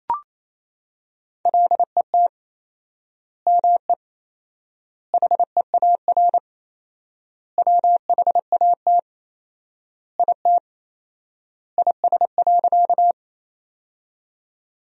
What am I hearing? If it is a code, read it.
Morse code, heard as a keyed tone, about 28 wpm, 700 Hz. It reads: LET ME HEAR WHAT IT IS.
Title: 4x Word-Spacing - Sentences from Top 300 Words - 28wpm